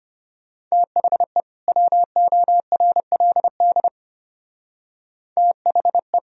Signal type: Morse code

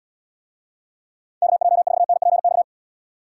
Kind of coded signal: Morse code

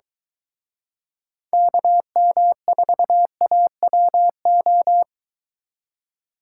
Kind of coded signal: Morse code

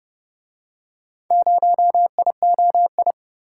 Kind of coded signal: Morse code

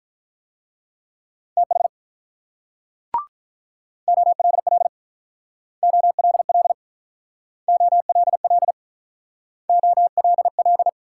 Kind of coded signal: Morse code

